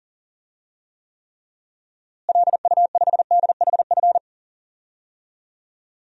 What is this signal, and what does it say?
Morse code, heard as a keyed tone, about 40 wpm, 700 Hz. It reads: LU5DHF
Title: DXCC-100 Call Signs - 6 Characters - Repeated 3x - 40wpm